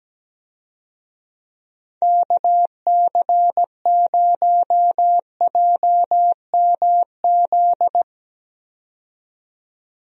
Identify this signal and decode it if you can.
Morse code, heard as a keyed tone, about 17 wpm, 700 Hz. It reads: KC0JMZ
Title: USA Call Signs - 6 Characters - 17wpm